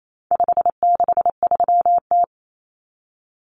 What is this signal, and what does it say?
Morse code, heard as a keyed tone, about 28 wpm, 700 Hz. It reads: 563T
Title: Common Cut Numbers - 4-Digit 28wpm